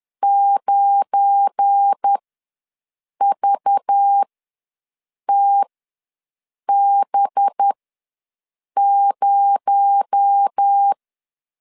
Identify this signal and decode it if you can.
Morse code, heard as a keyed tone, about 10 wpm, spaced as for 6 wpm, 800 Hz. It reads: 9VTB0
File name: LEZIONI / 7 GF9Z / 7 GF9 ws 10wpm 4034